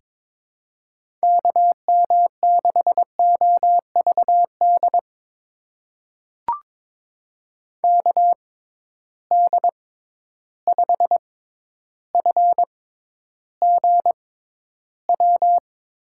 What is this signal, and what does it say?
Morse code, heard as a keyed tone, about 22 wpm, spaced as for 11 wpm, 700 Hz. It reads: KM6OVD KD5FGW
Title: USA Call Signs - 6 Characters - 6x Character Spacing - 22wpm